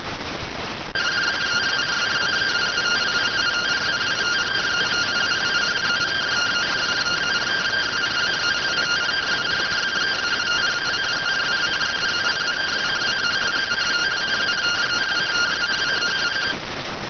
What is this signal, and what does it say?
Two-tone teleprinter signal, about 170 Hz shift.